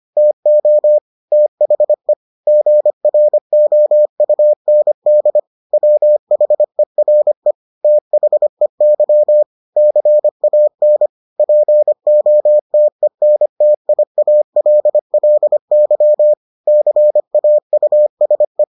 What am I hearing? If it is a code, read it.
Morse code, heard as a keyed tone, about 25 wpm, 600 Hz. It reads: TO THE GROUND WHERE THEY CAN POTENTIALLY CAUSE